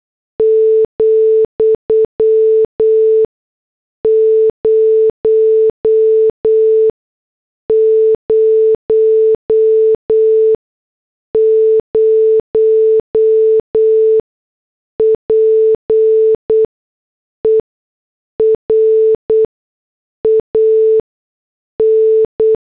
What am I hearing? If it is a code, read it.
Morse code, heard as a keyed tone, about 8 wpm, spaced as for 6 wpm, 440 Hz. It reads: ,000 PER AN